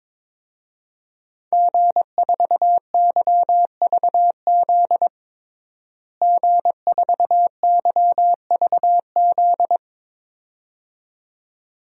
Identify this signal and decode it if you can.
Morse code, heard as a keyed tone, about 21 wpm, 700 Hz. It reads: G4YVZ G4YVZ
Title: DXCC-100 Call Signs - 5 Characters - Repeated 2x - 22wpm